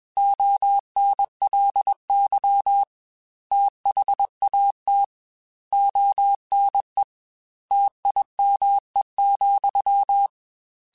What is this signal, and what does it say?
Morse code, heard as a keyed tone, about 21 wpm, 800 Hz. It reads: ONLY THAT ONE TIME,
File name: Technical Subjects / Technical Subjects-mp3s / AND Gate @20